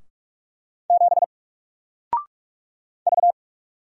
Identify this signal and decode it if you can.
Morse code, heard as a keyed tone, about 46 wpm, 700 Hz. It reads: 7 V
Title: Single Letter-Number 45wpm